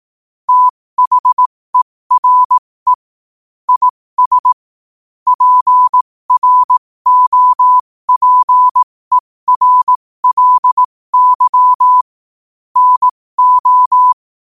Morse code, heard as a keyed tone, about 18 wpm, spaced as for 15 wpm, 1000 Hz. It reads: THERE IS PROPERLY NO
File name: qotdfiles / 15wpm